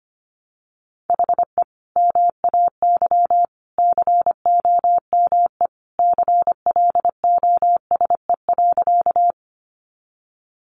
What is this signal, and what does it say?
Morse code, heard as a keyed tone, about 25 wpm, 700 Hz. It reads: HE MAY COME CLOSE.
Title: Sentences from Top 300 Words 25wpm